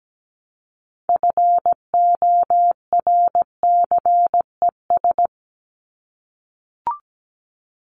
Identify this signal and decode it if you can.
Morse code, heard as a keyed tone, about 17 wpm, 700 Hz. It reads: FORCES
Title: ES:S Suffix - Top 100 Words with Base Words 17wpm